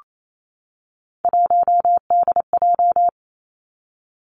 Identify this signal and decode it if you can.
Morse code, heard as a keyed tone, about 28 wpm, 700 Hz. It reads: 1DJ
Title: Three Challenging Letter-Number Combinations 28wpm